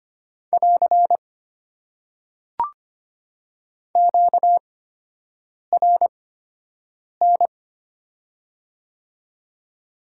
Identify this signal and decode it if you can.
Morse code, heard as a keyed tone, about 25 wpm, spaced as for 7 wpm, 700 Hz. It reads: <AR> QRN
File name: Top 90 QSO Elements - 8x Character Spacing - 25wpm